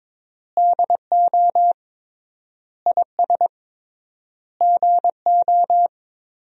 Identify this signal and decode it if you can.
Morse code, heard as a keyed tone, about 22 wpm, 700 Hz. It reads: DO IS GO
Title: 3x Word-Spacing - Sentences from Top 200 Words - 22wpm